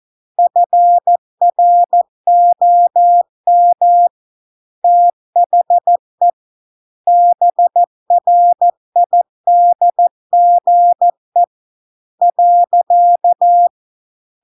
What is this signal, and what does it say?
Morse code, heard as a keyed tone, about 14 wpm, 700 Hz. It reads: FROM THE BRIDGE .